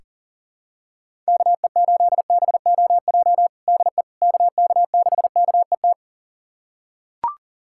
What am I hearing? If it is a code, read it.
Morse code, heard as a keyed tone, about 40 wpm, 700 Hz. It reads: KE8BOJ DE KK6KET
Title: USA Callsign DE Callsign - 6 Characters - 40wpm